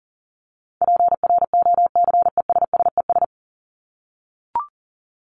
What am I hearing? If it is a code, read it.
Morse code, heard as a keyed tone, about 40 wpm, 700 Hz. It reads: PROCESSES
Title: ES:S Suffix - Top 100 Words with Base Words 40wpm